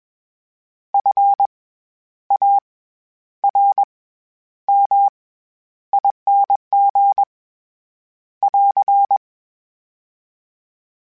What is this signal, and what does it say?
Morse code, heard as a keyed tone, about 21 wpm, 800 Hz. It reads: F A R M ING <AR>
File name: Special INGs-mp3s / Special INGs-mp3s / SlowToFaster ing @20